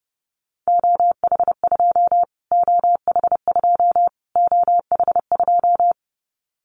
Morse code, heard as a keyed tone, about 28 wpm, 690 Hz. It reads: OH2 OH2 OH2
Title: Callsign Prefixes for Top 25 DXCC - Repeated 3x - 30wpm